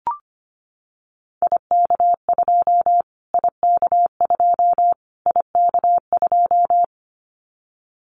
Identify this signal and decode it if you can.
Morse code, heard as a keyed tone, about 25 wpm, 700 Hz. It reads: IK2 IK2 IK2
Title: Callsign Prefixes for Top 25 DXCC - Repeated 3x - 25wpm